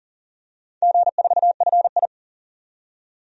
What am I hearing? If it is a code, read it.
Morse code, heard as a keyed tone, about 40 wpm, 700 Hz. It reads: G4FI
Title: DXCC-100 Call Signs - 4 Characters - 40wpm